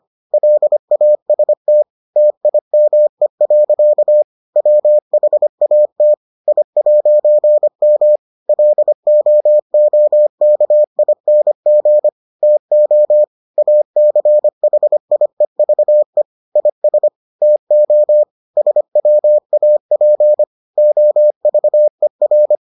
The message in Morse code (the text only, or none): LAST TIME. WHAT I'M LOOKING TO ACHIEVE IS TO SWAP OVER